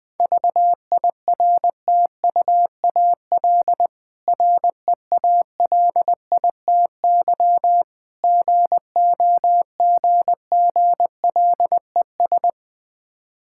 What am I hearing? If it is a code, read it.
Morse code, heard as a keyed tone, about 20 wpm, 700 Hz. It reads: VIRTUAL REALITY GOGGLES
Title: Difficult Words with Hints - 20wpm